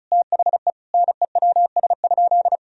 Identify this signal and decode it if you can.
Morse code, heard as a keyed tone, about 34 wpm, 700 Hz. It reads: THE NEWS?
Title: Sentences from Top 1000 Words - Review Part 4 - 35wpm